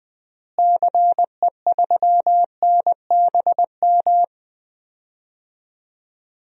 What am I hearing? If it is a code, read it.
Morse code, heard as a keyed tone, about 20 wpm, 700 Hz. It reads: CE3NBM
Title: DXCC-100 Call Signs - 6 Characters - Repeated 3x - 20wpm